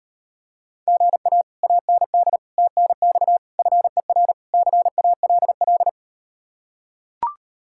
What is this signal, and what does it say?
Morse code, heard as a keyed tone, about 38 wpm, 700 Hz. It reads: GA AND TNX FER CALL